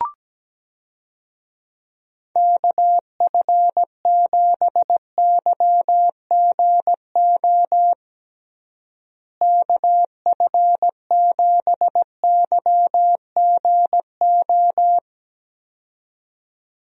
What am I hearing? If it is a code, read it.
Morse code, heard as a keyed tone, about 17 wpm, 700 Hz. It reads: KF7YGO KF7YGO
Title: USA Call Signs - 6 Characters - Repeated 2x - 17wpm